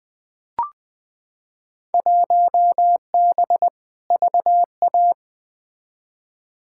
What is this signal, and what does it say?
Morse code, heard as a keyed tone, about 20 wpm, 700 Hz. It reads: 1B VA
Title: Field Day Exchanges Encore - 20wpm